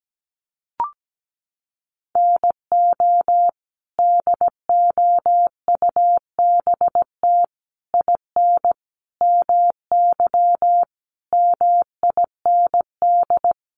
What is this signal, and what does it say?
Morse code, heard as a keyed tone, about 17 wpm, 700 Hz. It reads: NO DOUBT IN MY MIND
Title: Sets of 5 Words Encore2 17wpm